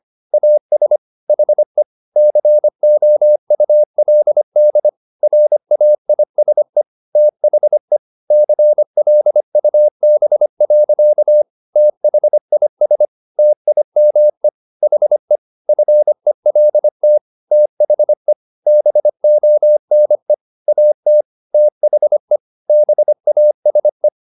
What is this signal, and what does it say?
Morse code, heard as a keyed tone, about 25 wpm, 600 Hz. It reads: AS HE COULD RAISE THE CLUB. THIS TIME HE FELT THE BONE AT THE BASE